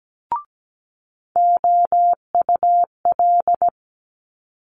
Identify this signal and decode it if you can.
Morse code, heard as a keyed tone, about 17 wpm, 700 Hz. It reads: OUL
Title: Multi-letter Phonemes Quick Fire 17wpm